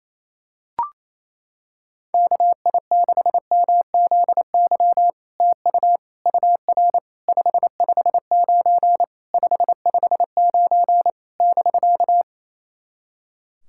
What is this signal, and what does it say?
Morse code, heard as a keyed tone, about 29 wpm, 700 Hz. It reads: KI6MZY TU UR 559 559 <BK>
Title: SOTA QSO Practice 28wpm